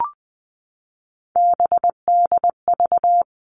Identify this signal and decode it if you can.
Morse code, heard as a keyed tone, about 20 wpm, 700 Hz. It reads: BDV
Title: All Cut Numbers - 3-Digit 20wpm